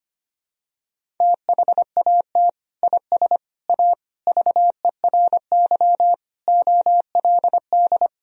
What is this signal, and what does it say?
Morse code, heard as a keyed tone, about 25 wpm, 700 Hz. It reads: THAT IS A VERY OLD
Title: Sentences from Top 1000 Words - Review Part 1 - 25wpm